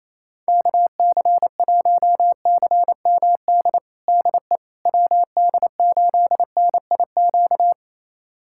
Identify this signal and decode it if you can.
Morse code, heard as a keyed tone, about 28 wpm, 700 Hz. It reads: KC1CMD DE WD8NIQ
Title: USA Callsign DE Callsign 28wpm